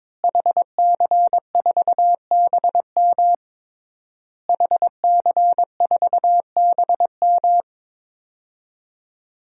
Morse code, heard as a keyed tone, about 22 wpm, 700 Hz. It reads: HC4BM HC4BM
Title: DXCC-100 Call Signs - 5 Characters - Repeated 3x - 22wpm